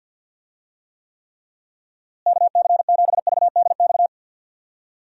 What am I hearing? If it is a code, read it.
Morse code, heard as a keyed tone, about 47 wpm, 700 Hz. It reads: KC7VDX
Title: USA Call Signs - 6 Characters - Repeated 3x - 50wpm